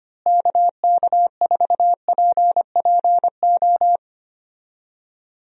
Morse code, heard as a keyed tone, about 25 wpm, 700 Hz. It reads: KK4PPO
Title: USA Call Signs - 6 Characters - Repeated 2x - 25wpm